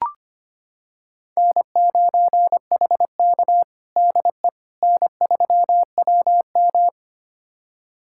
Morse code, heard as a keyed tone, about 25 wpm, 700 Hz. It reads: N9HK DE N3WM
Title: USA Callsign DE Callsign - 4 Characters - 25wpm